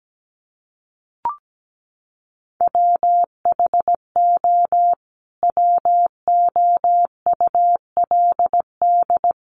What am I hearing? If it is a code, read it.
Morse code, heard as a keyed tone, about 17 wpm, 700 Hz. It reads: WHO WOULD
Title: Sentences from Top 1200 Words 17wpm